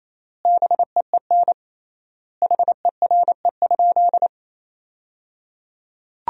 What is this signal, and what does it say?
Morse code, heard as a keyed tone, about 28 wpm, 700 Hz. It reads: BEEN HERE?
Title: 3x Word-Spacing - Sentences from Top 200 Words - 28wpm